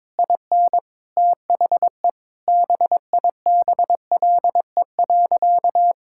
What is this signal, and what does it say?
Morse code, heard as a keyed tone, about 22 wpm, 700 Hz. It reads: IN THE BIBLE.
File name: Sentences with 9-Word Phrases - 22wpm